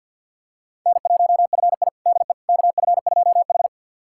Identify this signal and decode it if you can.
Morse code, heard as a keyed tone, about 51 wpm, 700 Hz. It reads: N1FI DE KF2H
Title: USA Callsign DE Callsign - 4 Characters - 50wpm